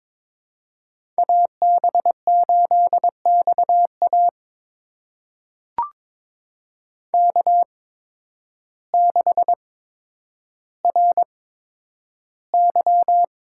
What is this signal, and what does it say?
Morse code, heard as a keyed tone, about 22 wpm, spaced as for 10 wpm, 700 Hz. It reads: AB8XA K6RY
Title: USA Call Signs - 5 Characters - 8x Character Spacing - 22wpm